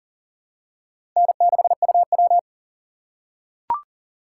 Morse code, heard as a keyed tone, about 40 wpm, 700 Hz. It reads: N6UW
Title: USA Call Signs - 4 Characters - 4x Character Spacing - 40wpm